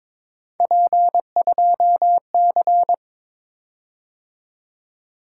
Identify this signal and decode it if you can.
Morse code, heard as a keyed tone, about 22 wpm, 700 Hz. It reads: P2C
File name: practice / 22wpm / Three Letter-Number Combinations - 22wpm